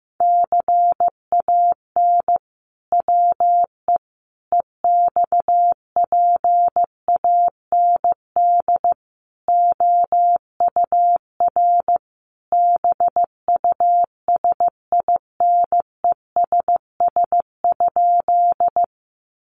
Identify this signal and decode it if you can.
Morse code, heard as a keyed tone, about 15 wpm, 700 Hz. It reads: CAN WE EXPAND OUR BUSINESS?
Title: Sentences from Top 1400 Words 15wpm